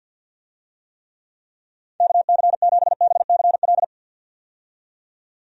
Morse code, heard as a keyed tone, about 46 wpm, 700 Hz. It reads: KC7BCL
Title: USA Call Signs - 6 Characters - 50wpm